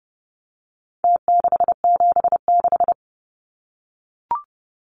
Morse code, heard as a keyed tone, about 30 wpm, 700 Hz. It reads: T676
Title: Numbers - Contest Style - 4-Digit 30wpm